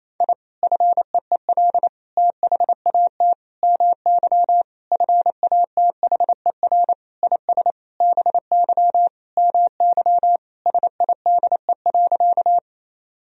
Morse code, heard as a keyed tone, about 28 wpm, 700 Hz. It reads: I FEEL THAT MY FATHER IS BY MY SIDE.
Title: Sentences from Top 1000 Words - Review Part 1 - 28wpm